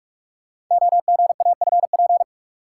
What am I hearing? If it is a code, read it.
Morse code, heard as a keyed tone, about 44 wpm, 700 Hz. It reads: OGAFP
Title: Code Groups - 5 Letters - 45wpm